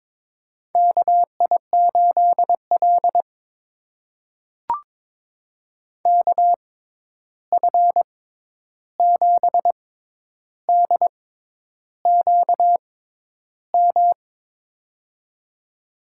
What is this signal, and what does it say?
Morse code, heard as a keyed tone, about 23 wpm, spaced as for 10 wpm, 700 Hz. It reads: KI8L KF7DQM
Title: USA Call Signs - 6x Character Spacing - 22wpm